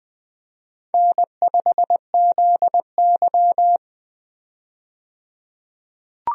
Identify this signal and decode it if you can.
Morse code, heard as a keyed tone, about 20 wpm, 700 Hz. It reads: N5ZY
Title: USA Call Signs - 4 Characters - 20wpm